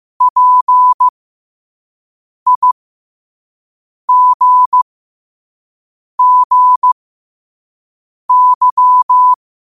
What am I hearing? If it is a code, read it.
Morse code, heard as a keyed tone, about 15 wpm, spaced as for 5 wpm, 1000 Hz. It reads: PIGGY
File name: qotdfiles / 5wpm